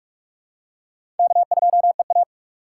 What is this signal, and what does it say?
Morse code, heard as a keyed tone, about 46 wpm, 700 Hz. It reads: K2EA
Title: USA Call Signs - 2x Character Spacing - 45wpm